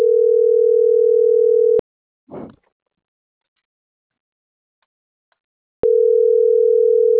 Ringback tone (North America), 440 + 480 Hz.